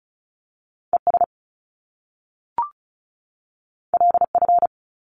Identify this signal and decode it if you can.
Morse code, heard as a keyed tone, about 35 wpm, 700 Hz. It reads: ES LF
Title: Top Two Letter Combinations in Words 35wpm